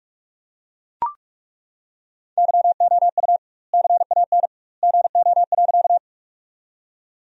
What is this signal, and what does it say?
Morse code, heard as a keyed tone, about 41 wpm, 700 Hz. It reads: YOU CAN GO.
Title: 2x Word-Spacing - Sentences from Top 100 Words - 45wpm